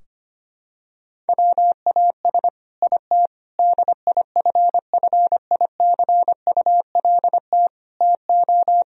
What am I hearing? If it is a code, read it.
Morse code, heard as a keyed tone, about 25 wpm, 700 Hz. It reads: WAS IT DIFFICULT TO
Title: Sentences from Top 1000 Words - Review Part 3 - 25wpm